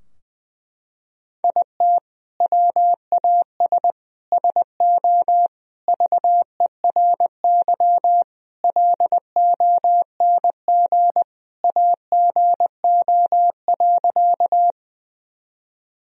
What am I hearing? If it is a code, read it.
Morse code, heard as a keyed tone, about 20 wpm, 700 Hz. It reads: IT WAS SO VERY LONG AGO.
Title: Sentences from Top 1000 Words - Review Part 3 - 20wpm